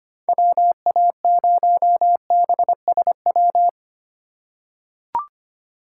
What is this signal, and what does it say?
Morse code, heard as a keyed tone, about 25 wpm, 700 Hz. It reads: WA0BSW
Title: USA Call Signs - 6 Characters - 2x Character Spacing - 25wpm